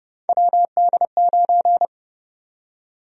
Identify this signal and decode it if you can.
Morse code, heard as a keyed tone, about 30 wpm, 700 Hz. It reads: WD9
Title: USA Call Sign Prefixes - Repeated 2x - 30wpm